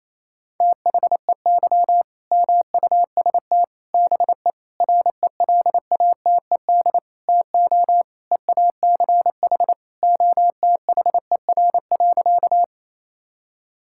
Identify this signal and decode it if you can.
Morse code, heard as a keyed tone, about 28 wpm, 700 Hz. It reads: THEY MUST BE RELATED TO EACH OTHER.
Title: Sentences from Top 1000 Words - Review Part 5 - 28wpm